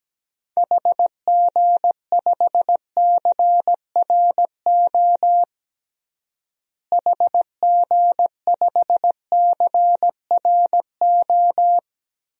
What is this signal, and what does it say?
Morse code, heard as a keyed tone, about 17 wpm, 700 Hz. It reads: HG5CRO HG5CRO
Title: DXCC-100 Call Signs - 6 Characters - Repeated 3x - 17wpm